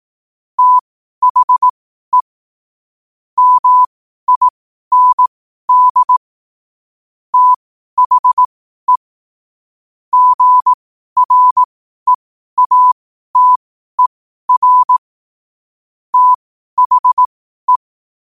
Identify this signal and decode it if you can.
Morse code, heard as a keyed tone, about 18 wpm, spaced as for 12 wpm, 1000 Hz. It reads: THE MIND THE GREATER THE
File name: qotdfiles / 12wpm